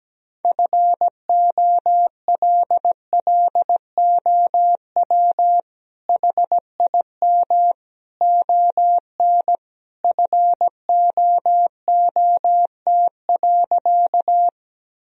Morse code, heard as a keyed tone, about 17 wpm, 700 Hz. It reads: FOLLOW HIM ON FOOT.
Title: Sentences from Top 1000 Words - Review Part 3 - 17wpm